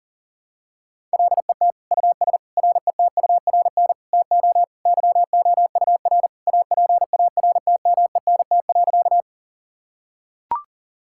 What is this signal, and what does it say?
Morse code, heard as a keyed tone, about 40 wpm, 700 Hz. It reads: LET US RETURN TO YOUR APARTMENT.
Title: Sentences from Top 1000 Words - Review Part 6 - 40wpm